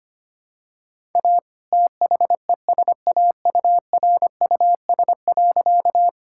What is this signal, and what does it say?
Morse code, heard as a keyed tone, about 25 wpm, 700 Hz. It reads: A THESAURUS.
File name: practice / 25wpm / All-Age Appropriate Jokes - 25wpm